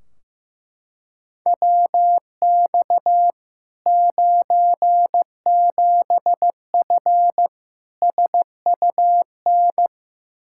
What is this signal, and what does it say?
Morse code, heard as a keyed tone, about 15 wpm, 700 Hz. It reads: WX 97F SUN
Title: Practice QSO - Exchange 2 - Info - 15wpm